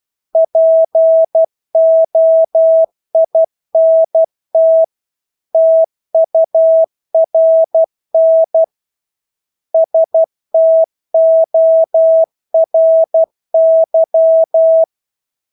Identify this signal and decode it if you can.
Morse code, heard as a keyed tone, about 12 wpm, 650 Hz. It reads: POINT TURN STORY
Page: Engelske ord 12 wpm